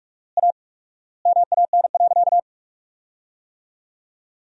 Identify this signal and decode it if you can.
Morse code, heard as a keyed tone, about 46 wpm, 700 Hz. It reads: A MAN.
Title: 4x Word-Spacing - Sentences from Top 300 Words - 45wpm